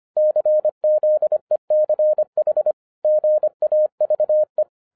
Morse code, heard as a keyed tone, about 25 wpm, 610 Hz. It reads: CZECH GAVE
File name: 2011-07-mp3 / n34 THE 3.11 2 25wpm